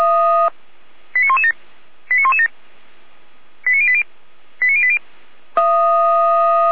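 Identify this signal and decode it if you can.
Five-tone selective call (ZVEI), heard as quick runs of five tones, tones of about 70 ms. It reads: 89117 89117 89080 89080
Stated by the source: ZVEI five-tone call 89092 received on an SDR.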